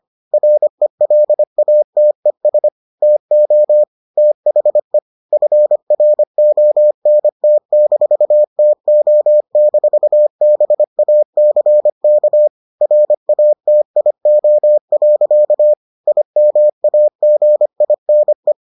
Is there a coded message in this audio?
Yes: Morse code from 0.3 to 18.5 s, heard as a keyed tone, about 25 wpm, 600 Hz. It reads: RELATES TO THE FRONT-TO-BACK RATIO. IMAGINE